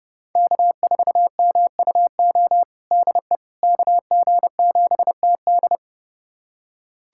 Morse code, heard as a keyed tone, about 29 wpm, 700 Hz. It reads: K4MUO DE KG7TD
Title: USA Callsign DE Callsign - 5 Characters - 30wpm